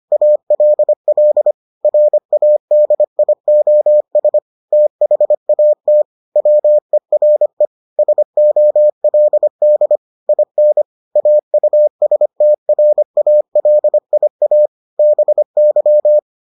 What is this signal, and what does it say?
Morse code, heard as a keyed tone, about 25 wpm, 600 Hz. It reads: ALL RADIOS THAT WERE SOLD IN AUSTRALIA BY